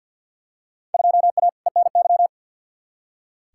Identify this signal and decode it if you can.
Morse code, heard as a keyed tone, about 46 wpm, 700 Hz. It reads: 2A ENY
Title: Field Day Exchanges Encore - 50wpm